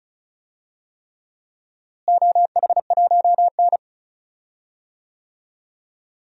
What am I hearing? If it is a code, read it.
Morse code, heard as a keyed tone, about 36 wpm, 700 Hz. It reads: OH1N
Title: DXCC-100 Call Signs - 4 Characters - Repeated 3x - 35wpm